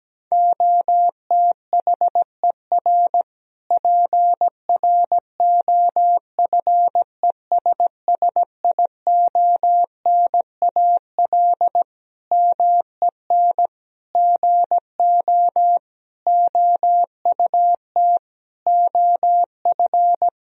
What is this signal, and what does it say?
Morse code, heard as a keyed tone, about 17 wpm, 700 Hz. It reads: OTHER PROFESSIONAL MEN GO OUT OF
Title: Sentences from Top 1000 Words - Review Part 4 - 17wpm